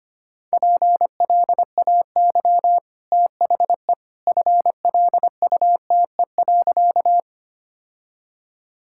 Morse code, heard as a keyed tone, about 25 wpm, 700 Hz. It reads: PLAY THE FLUTE.
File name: Easy Sentences with Hints - 25wpm